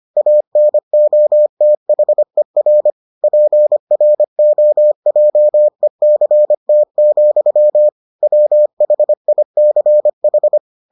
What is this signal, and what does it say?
Morse code, heard as a keyed tone, about 25 wpm, 600 Hz. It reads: ANOTHER PROJECT, WHICH